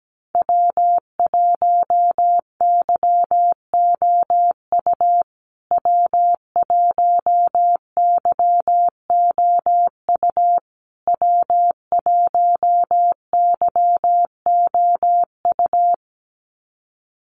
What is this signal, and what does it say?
Morse code, heard as a keyed tone, about 17 wpm, 700 Hz. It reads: W1YOU W1YOU W1YOU
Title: USA Call Signs Repeated 3x 17wpm